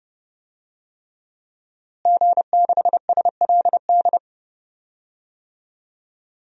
Morse code, heard as a keyed tone, about 32 wpm, 700 Hz. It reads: G6SLD
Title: DXCC-100 Call Signs - 5 Characters - Repeated 3x - 30wpm